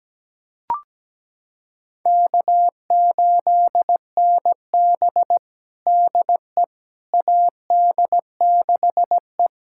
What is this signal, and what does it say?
Morse code, heard as a keyed tone, about 17 wpm, 700 Hz. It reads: K8NB DE AD6E
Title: USA Callsign DE Callsign - 4 Characters - 17wpm